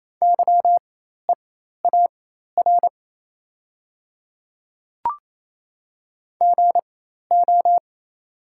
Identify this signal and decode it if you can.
Morse code, heard as a keyed tone, about 27 wpm, spaced as for 13 wpm, 700 Hz. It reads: YEAR GO
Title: 4x Character Spacing - Top 200 Words - 28wpm